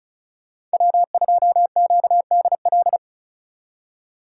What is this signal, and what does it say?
Morse code, heard as a keyed tone, about 34 wpm, 700 Hz. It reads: W2QDL
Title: USA Call Signs - 4x Character Spacing - 35wpm